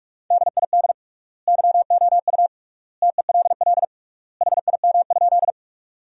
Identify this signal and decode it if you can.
Morse code, heard as a keyed tone, about 45 wpm, 700 Hz. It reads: DID YOU TELL HIM?
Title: 3x Word-Spacing - Sentences from Top 200 Words - 45wpm